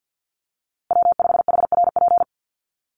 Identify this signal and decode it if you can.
Morse code, heard as a keyed tone, about 50 wpm, 700 Hz. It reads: W5HRP
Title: USA Call Signs Repeated 2x 50wpm